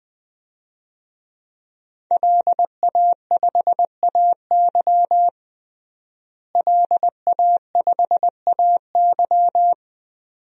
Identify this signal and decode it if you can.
Morse code, heard as a keyed tone, about 20 wpm, 700 Hz. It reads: LA5AY LA5AY
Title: DXCC-100 Call Signs - 5 Characters - Repeated 2x - 20wpm